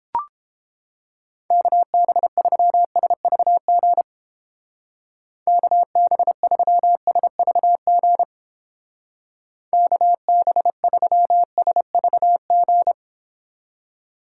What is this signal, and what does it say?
Morse code, heard as a keyed tone, about 29 wpm, 700 Hz. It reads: KB3SVG KB3SVG KB3SVG